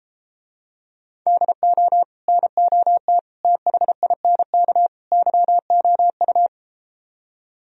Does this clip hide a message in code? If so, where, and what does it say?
Yes: Morse code from 1.3 to 6.5 s, heard as a keyed tone, about 33 wpm, 700 Hz. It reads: DO NOT THINK YOU